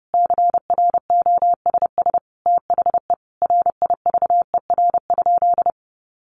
Morse code, heard as a keyed tone, about 30 wpm, 700 Hz. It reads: CROSS THE RIVER?
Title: Sentences from Top 1400 Words 30wpm